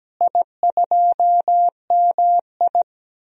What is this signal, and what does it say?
Morse code, heard as a keyed tone, about 17 wpm, 700 Hz. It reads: I2MI
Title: DXCC-100 Call Signs - 4 Characters - Repeated 2x - 17wpm